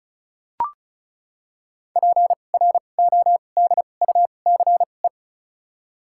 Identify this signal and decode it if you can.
Morse code, heard as a keyed tone, about 36 wpm, spaced as for 25 wpm, 700 Hz. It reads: PRODUCE
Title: 2x Character Spacing - Top 500 Words - 35wpm